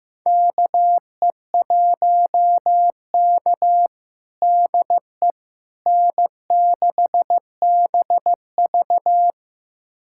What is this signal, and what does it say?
Morse code, heard as a keyed tone, about 15 wpm, 700 Hz. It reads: KE1K DE N6BV
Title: USA Callsign DE Callsign - 4 Characters - 15wpm